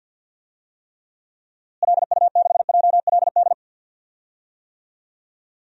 Morse code, heard as a keyed tone, about 52 wpm, 700 Hz. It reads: LU6JLD
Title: DXCC-100 Call Signs - 6 Characters - Repeated 3x - 50wpm